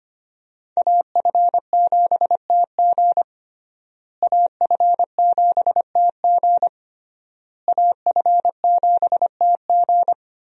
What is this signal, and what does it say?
Morse code, heard as a keyed tone, about 25 wpm, 700 Hz. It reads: AF7TG AF7TG AF7TG
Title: USA Call Signs - 5 Characters - Repeated 3x - 25wpm